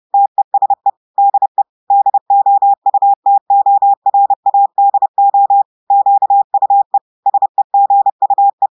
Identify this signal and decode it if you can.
Morse code, heard as a keyed tone, about 30 wpm, 800 Hz. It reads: TESE DE DOUTORADO QUE SEGUE
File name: audios / portugais / portugais texte en clair 1 vitesse 30 mots minutes